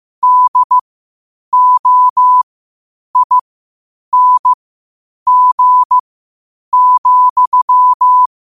Morse code, heard as a keyed tone, about 15 wpm, spaced as for 8 wpm, 1000 Hz. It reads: DOING,
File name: qotdfiles / 8wpm